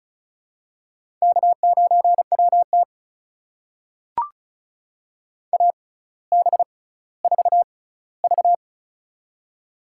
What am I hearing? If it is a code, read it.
Morse code, heard as a keyed tone, about 34 wpm, spaced as for 18 wpm, 700 Hz. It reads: K9WT AB4V
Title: USA Call Signs - 4 Characters - 6x Character Spacing - 35wpm